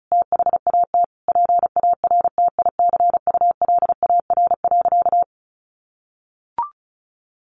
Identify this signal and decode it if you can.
Morse code, heard as a keyed tone, about 34 wpm, 700 Hz. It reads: THAT PARTICULAR.
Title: Sentences from Top 600 Words 35wpm